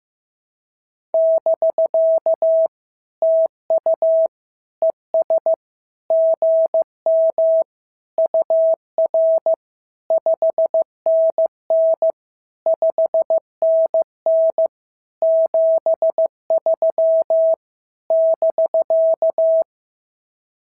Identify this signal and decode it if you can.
Morse code, heard as a keyed tone, about 15 wpm, 650 Hz. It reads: <BK> TU ES GM UR 5NN 5NN 73 <BK>
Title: SOTA QSO Practice 15wpm